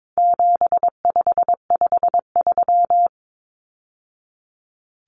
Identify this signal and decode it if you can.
Morse code, heard as a keyed tone, about 22 wpm, 700 Hz. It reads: 7553
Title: Numbers - 4-Digit 22wpm